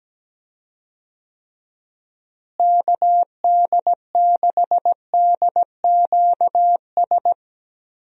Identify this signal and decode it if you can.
Morse code, heard as a keyed tone, about 17 wpm, 700 Hz. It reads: KD6DQS
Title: USA Call Signs - 6 Characters - Repeated 2x - 17wpm